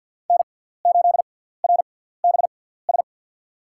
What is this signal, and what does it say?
Morse code, heard as a keyed tone, about 52 wpm, spaced as for 17 wpm, 700 Hz. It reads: N8RBS
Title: USA Call Signs - 5 Characters - 6x Character Spacing - 50wpm